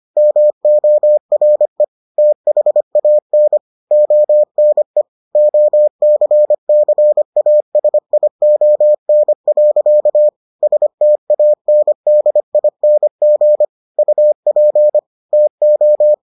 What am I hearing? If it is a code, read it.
Morse code, heard as a keyed tone, about 25 wpm, 600 Hz. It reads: MORE THAN ONE OCCASION. STANDING UP TO